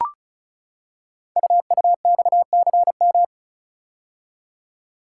Morse code, heard as a keyed tone, about 36 wpm, 690 Hz. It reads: UUXCM
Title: Code Groups - 5 Letters - 35wpm